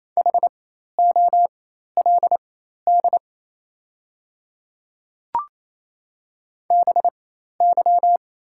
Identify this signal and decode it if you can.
Morse code, heard as a keyed tone, about 28 wpm, spaced as for 13 wpm, 700 Hz. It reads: HOLD BY
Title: 4x Character Spacing - Top 500 Words - 28wpm